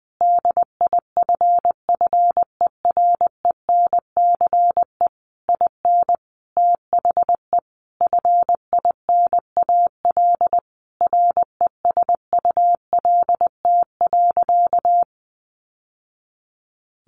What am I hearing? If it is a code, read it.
Morse code, heard as a keyed tone, about 20 wpm, 700 Hz. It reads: DIFFERENCE IN THE FINAL RESULT.
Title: Sentences from Top 800 Words 20wpm